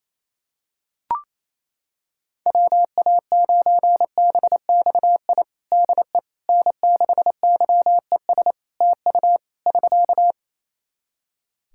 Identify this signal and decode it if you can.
Morse code, heard as a keyed tone, about 28 wpm, 700 Hz. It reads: WA9BXI DE N6YES TU <SK>
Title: Practice QSO - Exchange 4 - Ending - 28wpm